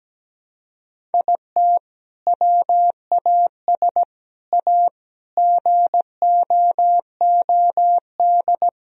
Morse code, heard as a keyed tone, about 17 wpm, 700 Hz. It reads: IT WAS A GOOD
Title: Sentences from Top 1000 Words - Review Part 1 - 17wpm